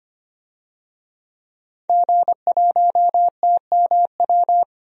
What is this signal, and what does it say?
Morse code, heard as a keyed tone, about 25 wpm, 700 Hz. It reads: G1TMW
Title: DXCC-100 Call Signs - 5 Characters - 25wpm